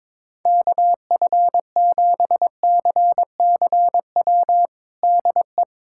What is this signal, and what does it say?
Morse code, heard as a keyed tone, about 22 wpm, 700 Hz. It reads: KF7CCW DE
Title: USA Callsign DE Callsign - 6 Characters - 22wpm